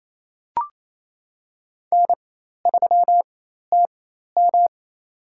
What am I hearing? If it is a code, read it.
Morse code, heard as a keyed tone, about 28 wpm, spaced as for 13 wpm, 700 Hz. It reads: N3TM
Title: USA Call Signs - 4 Characters - 4x Character Spacing - 28wpm